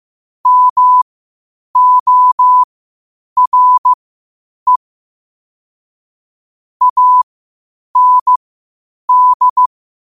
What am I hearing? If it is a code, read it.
Morse code, heard as a keyed tone, about 15 wpm, spaced as for 8 wpm, 1000 Hz. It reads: MORE AND